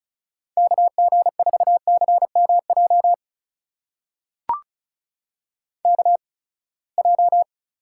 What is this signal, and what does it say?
Morse code, heard as a keyed tone, about 36 wpm, spaced as for 24 wpm, 700 Hz. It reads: KG4CMJ KJ
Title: USA Call Signs - 6 Characters - 8x Character Spacing - 35wpm